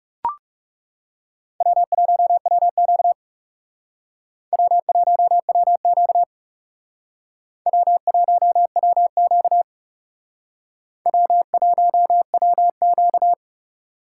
Morse code, heard as a keyed tone, about 37 wpm, 700 Hz. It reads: W1WQ W1WQ W1WQ W1WQ
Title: Speed Racing - USA 1x2 Call Signs 30wpm